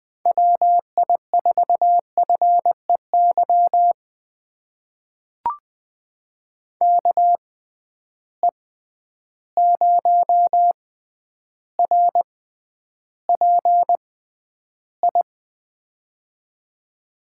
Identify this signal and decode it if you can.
Morse code, heard as a keyed tone, about 20 wpm, spaced as for 10 wpm, 700 Hz. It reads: WI4FEY KE0RPI
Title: USA Call Signs - 6 Characters - 6x Character Spacing - 20wpm